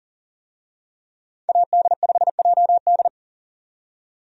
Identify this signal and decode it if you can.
Morse code, heard as a keyed tone, about 39 wpm, 700 Hz. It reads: AD5JD
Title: USA Call Signs - 5 Characters - 40wpm